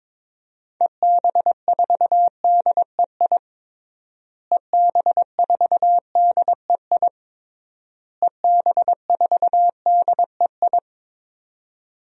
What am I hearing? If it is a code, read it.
Morse code, heard as a keyed tone, about 22 wpm, 700 Hz. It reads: EB4DEI EB4DEI EB4DEI
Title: DXCC-100 Call Signs - 6 Characters - Repeated 3x - 22wpm